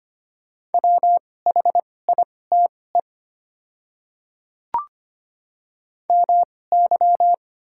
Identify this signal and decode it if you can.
Morse code, heard as a keyed tone, about 25 wpm, spaced as for 18 wpm, 700 Hz. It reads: WHITE MY
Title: 2x Character Spacing - Top 300 Words - 25wpm